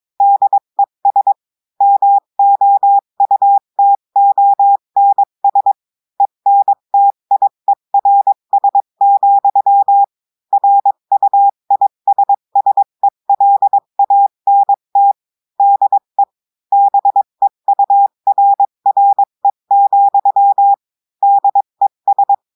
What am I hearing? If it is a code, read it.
Morse code, heard as a keyed tone, about 22 wpm, 800 Hz. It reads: DES MOUTONS ENTIERS, RUISSELANT DE BEURRE, DES
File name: cours-du-mercredi / audios / cours-853-du-2-mai-2024-etxte chiffre et texte clair-a-20-wpm